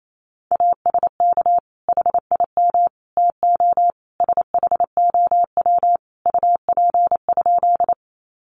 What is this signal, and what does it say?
Morse code, heard as a keyed tone, about 28 wpm, 700 Hz. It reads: ASK HIM TO SHOW UP?
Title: Sentences from Top 200 Words 28wpm